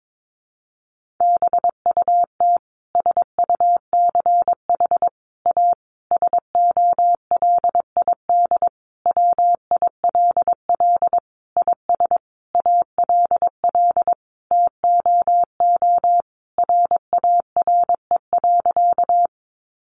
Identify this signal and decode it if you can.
Morse code, heard as a keyed tone, about 22 wpm, 700 Hz. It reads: BUT SUCH A SOLID WILL IS ALL TOO RARE.
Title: Sentences from Top 1500 Words 22wpm